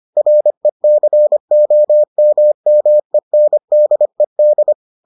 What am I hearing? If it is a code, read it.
Morse code, heard as a keyed tone, about 25 wpm, 600 Hz. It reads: RECOMMENDED